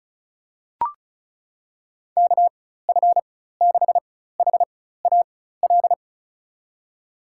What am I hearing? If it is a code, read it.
Morse code, heard as a keyed tone, about 36 wpm, spaced as for 16 wpm, 700 Hz. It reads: KF6HAL
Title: USA Call Signs - 6 Characters - 4x Character Spacing - 35wpm